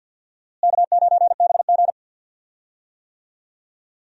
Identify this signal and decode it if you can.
Morse code, heard as a keyed tone, about 47 wpm, 700 Hz. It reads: K9BG
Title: USA Call Signs - 4 Characters - Repeated 2x - 50wpm